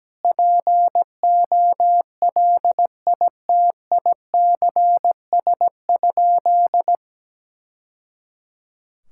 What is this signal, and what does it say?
Morse code, heard as a keyed tone, about 17 wpm, 700 Hz. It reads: POLITICS?
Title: Sentences from Top 1000 Words - Review Part 6 - 17wpm